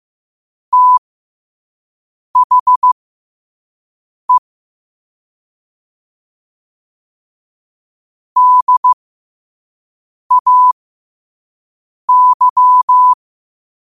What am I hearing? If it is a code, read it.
Morse code, heard as a keyed tone, about 15 wpm, spaced as for 5 wpm, 1000 Hz. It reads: THE DAY
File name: qotdfiles / 5wpm